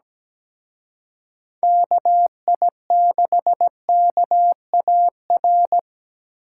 Morse code, heard as a keyed tone, about 20 wpm, 700 Hz. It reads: KI6KAR
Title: Speed Racing - USA 2x3 Call Signs 17wpm